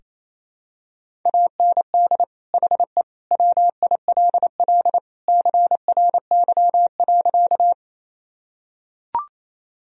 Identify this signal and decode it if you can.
Morse code, heard as a keyed tone, about 27 wpm, 700 Hz. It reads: AND HE WILL CRY.
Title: Sentences from Top 1000 Words - Review Part 2 - 28wpm